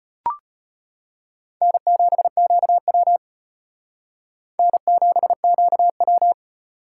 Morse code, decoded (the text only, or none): N7QW N7QW